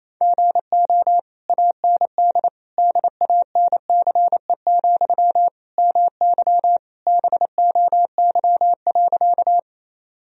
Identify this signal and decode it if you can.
Morse code, heard as a keyed tone, about 28 wpm, 700 Hz. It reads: GO AND DANCE, MY BOY.
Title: Sentences from Top 1000 Words - Review Part 4 - 28wpm